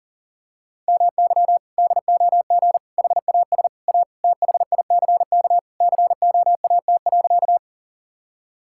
Morse code, heard as a keyed tone, about 37 wpm, 700 Hz. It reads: MY DOG HAS A THICK COAT.
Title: Sentences from Top 1600 Words 40wpm